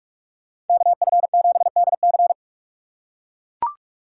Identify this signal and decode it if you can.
Morse code, heard as a keyed tone, about 44 wpm, 700 Hz. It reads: KF7DC